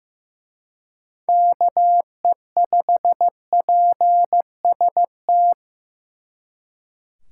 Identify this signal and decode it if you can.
Morse code, heard as a keyed tone, about 18 wpm, 700 Hz. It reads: KE5PST
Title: Speed Racing - USA Call Signs 15wpm